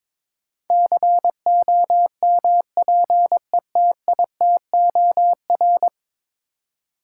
Morse code, heard as a keyed tone, about 22 wpm, 700 Hz. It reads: COMPETITOR
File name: practice / 22wpm / 10 Letter Words - First 3 Letters Given - 22wpm